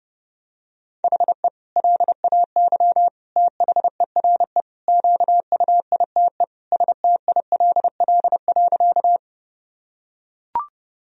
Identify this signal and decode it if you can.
Morse code, heard as a keyed tone, about 32 wpm, 700 Hz. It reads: HE LAY THERE QUITE STILL.
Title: Sentences from Top 1000 Words - Review Part 2 - 30wpm